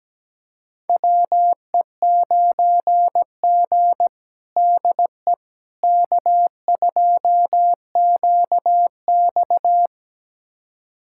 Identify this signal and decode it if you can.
Morse code, heard as a keyed tone, about 17 wpm, 700 Hz. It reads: WE9G DE K2QX
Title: USA Callsign DE Callsign - 4 Characters - 17wpm